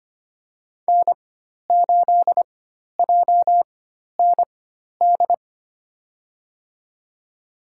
Morse code, heard as a keyed tone, about 25 wpm, spaced as for 12 wpm, 700 Hz. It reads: N8JND
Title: USA Call Signs - 5 Characters - 4x Character Spacing - 25wpm